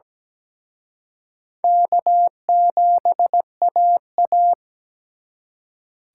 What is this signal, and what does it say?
Morse code, heard as a keyed tone, about 20 wpm, 700 Hz. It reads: K7AA
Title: Speed Racing - USA Call Signs 17wpm